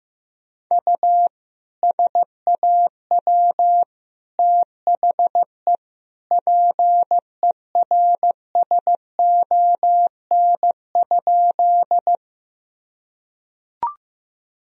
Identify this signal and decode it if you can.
Morse code, heard as a keyed tone, about 15 wpm, 700 Hz. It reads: U SAW THE PERSON?
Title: Sentences from Top 400 Words and Abbreviations 15wpm